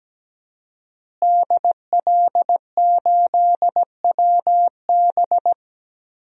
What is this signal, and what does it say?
Morse code, heard as a keyed tone, about 17 wpm, 700 Hz. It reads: DL8WB
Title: DXCC-100 Call Signs - 5 Characters - Repeated 2x - 17wpm